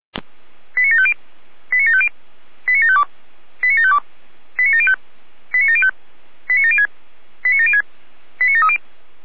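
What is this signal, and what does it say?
Five-tone selective call (ZVEI), heard as quick runs of five tones, tones of about 70 ms. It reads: 89750 89750 89752 89752 89705 89705 89706 89706 89730
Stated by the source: ZVEI five-tone selective call received with an SDR.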